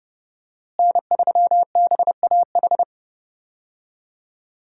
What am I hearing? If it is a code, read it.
Morse code, heard as a keyed tone, about 30 wpm, 700 Hz. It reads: N3BAH
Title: USA Call Signs - 5 Characters - Repeated 2x - 30wpm